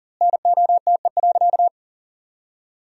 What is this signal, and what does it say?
Morse code, heard as a keyed tone, about 40 wpm, 700 Hz. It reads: NOTE.
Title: 2x Word-Spacing - Sentences from Top 500 Words - 40wpm